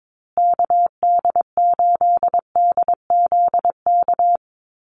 Morse code, heard as a keyed tone, about 22 wpm, 700 Hz. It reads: KD8DZK
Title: USA Call Signs Repeated 2x 22wpm